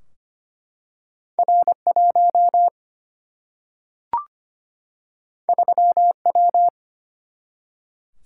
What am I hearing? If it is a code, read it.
Morse code, heard as a keyed tone, about 25 wpm, 700 Hz. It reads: R1 3W
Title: Single-Letter Single-Number 25wpm